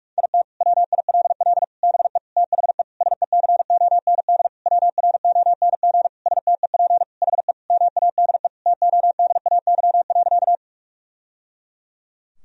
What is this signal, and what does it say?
Morse code, heard as a keyed tone, about 46 wpm, 700 Hz. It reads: IT WILL BE THE SECOND WRONG STEP HE MADE TODAY.
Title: Sentences from Top 1000 Words - Review Part 2 - 45wpm